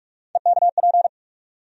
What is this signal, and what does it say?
Morse code, heard as a keyed tone, about 44 wpm, 700 Hz. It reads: EKP
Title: USA Call Sign Postfixes - 3 Letters 45wpm